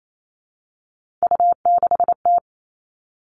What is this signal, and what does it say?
Morse code, heard as a keyed tone, about 28 wpm, 700 Hz. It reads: U6T
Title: Three Challenging Letter-Number Combinations 28wpm